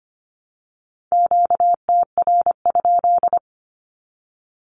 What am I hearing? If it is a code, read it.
Morse code, heard as a keyed tone, about 25 wpm, 700 Hz. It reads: QTR?
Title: Q-Codes Repeated 3x 25wpm